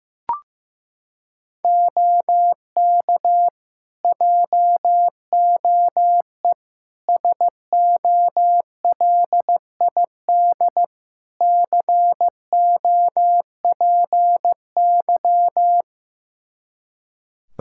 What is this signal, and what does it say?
Morse code, heard as a keyed tone, about 15 wpm, 700 Hz. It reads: OK JOE SOLID COPY
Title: K1USN SST QSO Practice 15wpm